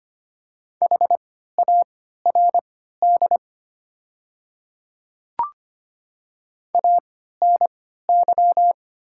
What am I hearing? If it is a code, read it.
Morse code, heard as a keyed tone, about 25 wpm, spaced as for 14 wpm, 690 Hz. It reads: HARD ANY
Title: 3x Character Spacing - Top 300 Words - 25wpm